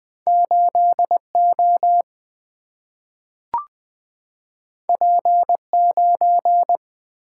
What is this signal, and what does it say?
Morse code, heard as a keyed tone, about 20 wpm, 700 Hz. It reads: 8O P9
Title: Single-Letter Single-Number 20wpm